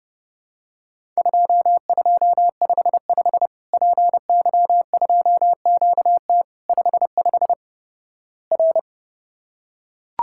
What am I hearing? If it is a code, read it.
Morse code, heard as a keyed tone, about 32 wpm, 700 Hz. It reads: 2255 PY2QT 55 R
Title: Worked All Europe DX Contest - QTC Exchange - 30wpm